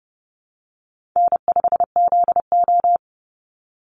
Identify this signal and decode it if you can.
Morse code, heard as a keyed tone, about 30 wpm, 700 Hz. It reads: N5ZO
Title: Call Signs Repeated 3x - 30wpm